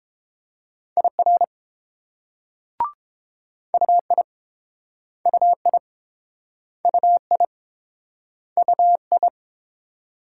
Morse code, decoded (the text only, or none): IR UI UI UI UI